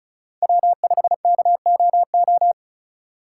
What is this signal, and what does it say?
Morse code, heard as a keyed tone, about 34 wpm, 700 Hz. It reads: W5KOO
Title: USA Call Signs - 5 Characters - 4x Character Spacing - 35wpm